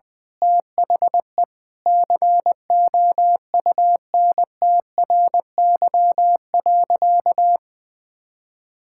Morse code, heard as a keyed tone, about 20 wpm, 700 Hz. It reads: THE COUNTRY.
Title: Sentences from Top 1000 Words - Review Part 5 - 20wpm